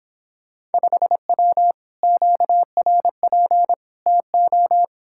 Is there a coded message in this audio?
Yes: Morse code from 0.7 to 4.8 s, heard as a keyed tone, about 31 wpm, 700 Hz. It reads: 5W QRP TO